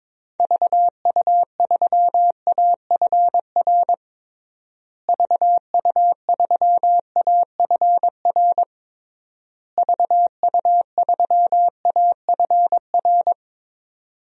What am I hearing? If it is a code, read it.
Morse code, heard as a keyed tone, about 21 wpm, 700 Hz. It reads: VU3AFR VU3AFR VU3AFR
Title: DXCC-100 Call Signs - 6 Characters - Repeated 3x - 22wpm